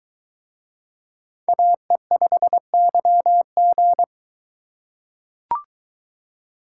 Morse code, heard as a keyed tone, about 22 wpm, 700 Hz. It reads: AE5YG